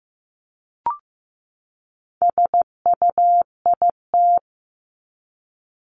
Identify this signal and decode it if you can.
Morse code, heard as a keyed tone, about 15 wpm, 700 Hz. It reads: SUIT
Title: Top 1501-1750 Words 15wpm